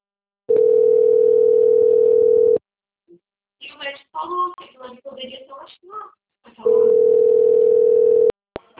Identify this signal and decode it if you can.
Ringback tone (North America), 440 + 480 Hz, with speech.
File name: out- - - - - .11994